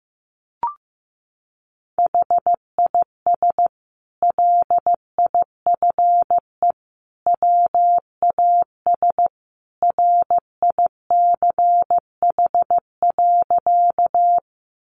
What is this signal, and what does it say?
Morse code, heard as a keyed tone, about 15 wpm, 700 Hz. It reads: HIS LIFE WAS RICH.
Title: Sentences from Top 1100 Words 15wpm